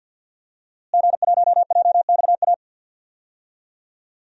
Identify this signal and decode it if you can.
Morse code, heard as a keyed tone, about 53 wpm, 700 Hz. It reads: G1JXA
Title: DXCC-100 Call Signs - 5 Characters - Repeated 2x - 50wpm